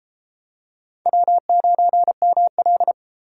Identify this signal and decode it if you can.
Morse code, heard as a keyed tone, about 33 wpm, 700 Hz. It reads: W9ML